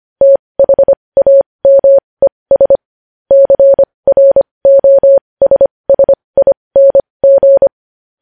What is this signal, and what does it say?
Morse code, heard as a keyed tone, about 25 wpm, spaced as for 20 wpm, 550 Hz. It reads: THAMES CROSSING
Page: news-headlines-in-morse-code-at-20-wpm